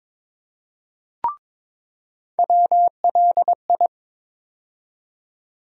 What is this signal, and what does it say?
Morse code, heard as a keyed tone, about 21 wpm, 700 Hz. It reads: WLI
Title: USA Call Sign Postfixes - 3 Letters 22wpm